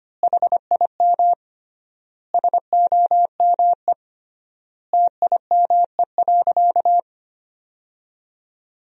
Morse code, heard as a keyed tone, about 25 wpm, 700 Hz. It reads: HIM SOME TIME.
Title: 3x Word-Spacing - Sentences from Top 300 Words - 25wpm